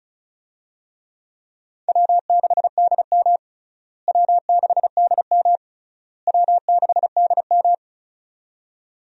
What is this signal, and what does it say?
Morse code, heard as a keyed tone, about 34 wpm, 700 Hz. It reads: W6DM W6DM W6DM
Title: USA Call Signs - 4 Characters - Repeated 3x - 35wpm